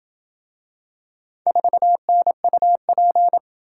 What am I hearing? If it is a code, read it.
Morse code, heard as a keyed tone, about 26 wpm, 700 Hz. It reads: 4NUP